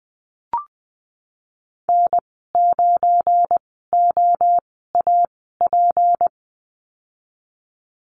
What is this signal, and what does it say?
Morse code, heard as a keyed tone, about 20 wpm, spaced as for 15 wpm, 700 Hz. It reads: N9OAP
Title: 193 Lesson 31g - Letter v - Call Signs 20wpm